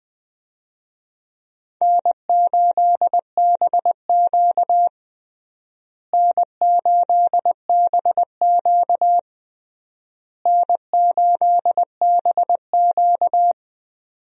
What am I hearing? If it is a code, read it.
Morse code, heard as a keyed tone, about 20 wpm, 700 Hz. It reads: N8BQ N8BQ N8BQ
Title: USA Call Signs - 4 Characters - Repeated 3x - 20wpm